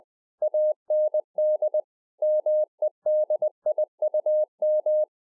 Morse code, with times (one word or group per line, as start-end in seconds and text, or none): 0.4-1.8 AND
2.2-5.0 MEDIUM